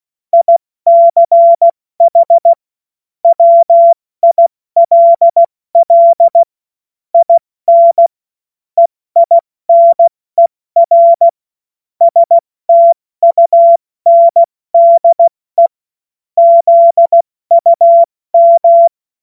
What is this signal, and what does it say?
Morse code, heard as a keyed tone, about 16 wpm, 680 Hz. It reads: ICH WILL IN EINER STUNDE ZUM
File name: Morsekurs / Witze / Witz 25. Juli 2023 70 F80